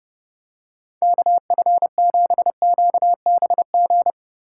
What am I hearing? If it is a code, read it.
Morse code, heard as a keyed tone, about 30 wpm, 700 Hz. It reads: KF7QBG
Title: USA Call Signs - 6 Characters - 3x Character Spacing - 30wpm